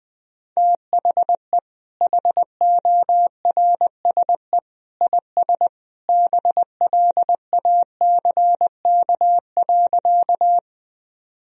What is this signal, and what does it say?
Morse code, heard as a keyed tone, about 20 wpm, 700 Hz. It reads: THE HORSE IS BLACK.
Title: Sentences from Top 1000 Words - Review Part 2 - 20wpm